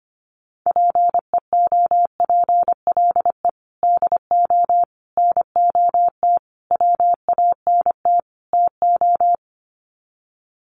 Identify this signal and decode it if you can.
Morse code, heard as a keyed tone, about 25 wpm, 700 Hz. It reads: PEOPLE DO NOT WANT TO
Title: Sets of 5 Words Encore2 25wpm